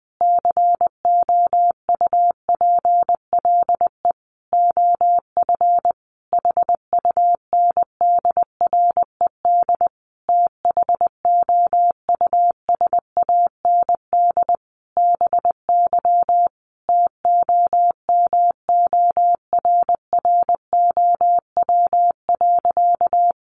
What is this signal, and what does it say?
Morse code, heard as a keyed tone, about 19 wpm, 700 Hz. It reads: COUPLE OF HUNDRED THOUSAND BY TOMORROW.
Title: Sentences from Top 1100 Words 20wpm